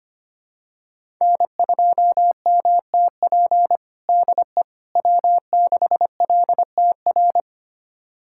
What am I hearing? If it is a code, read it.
Morse code, heard as a keyed tone, about 25 wpm, 700 Hz. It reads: N2MTP DE W6LTR
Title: USA Callsign DE Callsign - 5 Characters - 25wpm